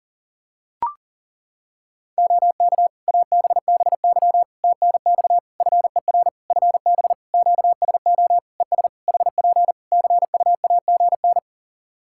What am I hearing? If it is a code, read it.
Morse code, heard as a keyed tone, about 37 wpm, 700 Hz. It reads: OK ABBY TNX FER FB QSO ES HP CUAGN
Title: Practice QSO - Exchange 4 - Ending - 40wpm